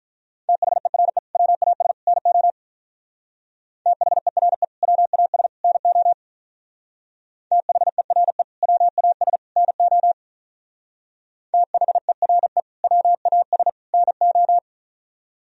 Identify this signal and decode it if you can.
Morse code, heard as a keyed tone, about 44 wpm, 700 Hz. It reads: THERE WAS NO THERE WAS NO THERE WAS NO THERE WAS NO